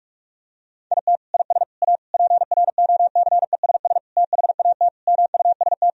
Morse code, heard as a keyed tone, about 41 wpm, 700 Hz. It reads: IT IS A PROCESS THAT MUST
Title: Sentences from Top 1000 Words - Review Part 3 - 45wpm